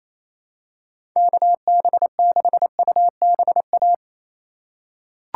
Morse code, heard as a keyed tone, about 27 wpm, 700 Hz. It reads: KB6UBA
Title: USA Call Signs - 2x Character Spacing - 28wpm